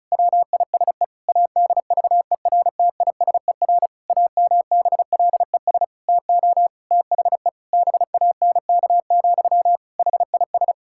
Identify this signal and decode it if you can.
Morse code, heard as a keyed tone, about 34 wpm, 700 Hz. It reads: WISE ADVERTISER AMBLES TO THE BANK, HIS
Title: Pangram Challenge v2 - 35wpm